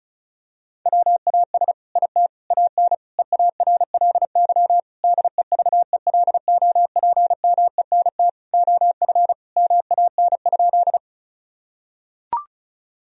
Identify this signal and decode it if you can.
Morse code, heard as a keyed tone, about 36 wpm, 700 Hz. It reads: WAS IT AN EARLY DEVELOPMENT OF MAN?
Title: Sentences from Top 1000 Words - Review Part 4 - 35wpm